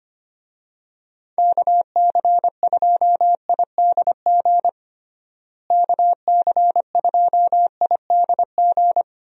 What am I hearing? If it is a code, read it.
Morse code, heard as a keyed tone, about 25 wpm, 700 Hz. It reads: KC2IDG KC2IDG
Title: USA Call Signs - 6 Characters - Repeated 2x - 25wpm